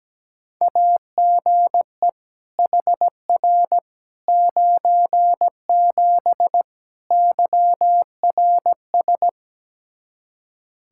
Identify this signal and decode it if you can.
Morse code, heard as a keyed tone, about 17 wpm, 700 Hz. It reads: AGE HR 97 YRS
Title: Practice QSO - Exchange 3 - More Info - 17wpm